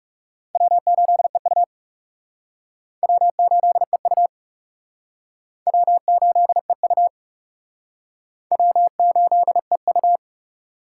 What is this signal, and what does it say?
Morse code, heard as a keyed tone, about 37 wpm, 700 Hz. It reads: W8EU W8EU W8EU W8EU
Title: Speed Racing - USA Call Signs 30wpm